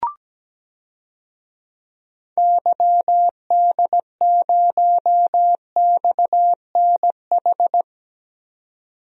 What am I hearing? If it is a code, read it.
Morse code, heard as a keyed tone, about 17 wpm, 700 Hz. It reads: YD0XNH
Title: DXCC-100 Call Signs - 6 Characters - 17wpm